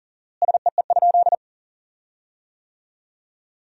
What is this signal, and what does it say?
Morse code, heard as a keyed tone, about 37 wpm, 700 Hz. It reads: SEE?
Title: 4x Word-Spacing - Sentences from Top 100 Words - 40wpm